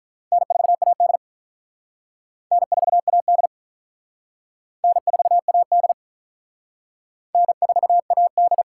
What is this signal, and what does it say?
Morse code, heard as a keyed tone, about 48 wpm, 700 Hz. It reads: N4AD N4AD N4AD N4AD